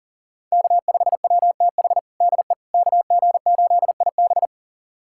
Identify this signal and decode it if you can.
Morse code, heard as a keyed tone, about 40 wpm, 700 Hz. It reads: K5WTH DE KG8IB
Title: USA Callsign DE Callsign - 5 Characters - 40wpm